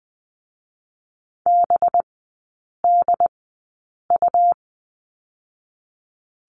Morse code, heard as a keyed tone, about 19 wpm, 700 Hz. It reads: B D U
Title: 205 Lesson 33e - Letter j - Three Characters 20wpm